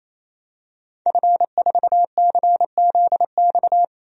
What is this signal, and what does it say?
Morse code, heard as a keyed tone, about 27 wpm, 700 Hz. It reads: F4CZX
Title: DXCC-100 Call Signs - 5 Characters - 28wpm